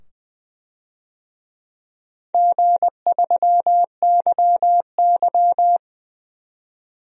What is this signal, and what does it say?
Morse code, heard as a keyed tone, about 20 wpm, 700 Hz. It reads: G3YY
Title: DXCC-100 Call Signs - 4 Characters - 20wpm